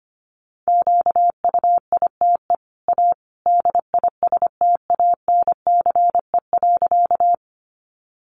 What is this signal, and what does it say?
Morse code, heard as a keyed tone, about 25 wpm, 700 Hz. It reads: QUITE A DISTANCE.
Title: Sentences from Top 1200 Words 25wpm